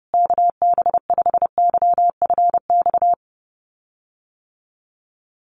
Morse code, heard as a keyed tone, about 30 wpm, 700 Hz. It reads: KB5YFX
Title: USA Call Signs Repeated 3x 30wpm